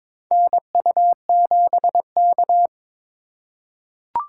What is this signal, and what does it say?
Morse code, heard as a keyed tone, about 22 wpm, 690 Hz. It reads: NU7K
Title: USA Call Signs - 4 Characters - 3x Character Spacing - 22wpm